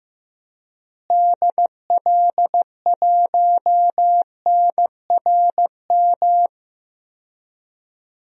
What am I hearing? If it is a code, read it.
Morse code, heard as a keyed tone, about 15 wpm, 700 Hz. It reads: DL1NRM
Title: DXCC-100 Call Signs - 6 Characters - Repeated 2x - 15wpm